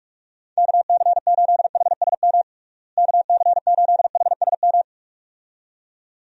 Morse code, heard as a keyed tone, about 41 wpm, 700 Hz. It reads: KC8HSM KC8HSM
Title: USA Call Signs - 6 Characters - Repeated 2x - 45wpm